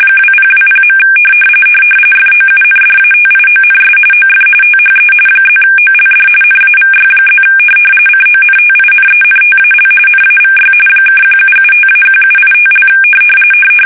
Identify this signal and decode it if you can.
RTTY, about 45.45 baud. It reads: 2020. FOUNDATIONS OF THE +OLD NAVY+ CONTINENTAL NAVY (1775-1785)
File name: fsk850 / usnavy1